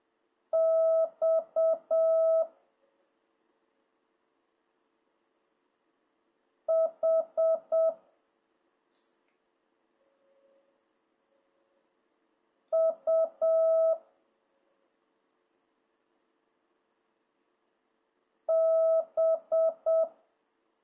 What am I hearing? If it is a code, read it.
Morse code, heard as a keyed tone, about 7 wpm, 650 Hz. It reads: X H U B